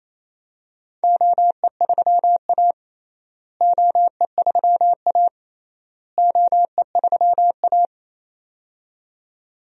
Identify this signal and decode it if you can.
Morse code, heard as a keyed tone, about 27 wpm, 700 Hz. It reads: OE3A OE3A OE3A
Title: DXCC-100 Call Signs - 4 Characters - Repeated 3x - 28wpm